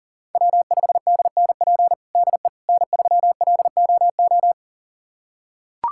Morse code, heard as a keyed tone, about 40 wpm, 700 Hz. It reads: W5DNP DE N3LOO
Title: USA Callsign DE Callsign - 5 Characters - 40wpm